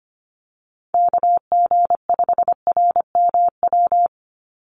Morse code, heard as a keyed tone, about 25 wpm, 700 Hz. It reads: KG5RMW
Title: USA Call Signs 25wpm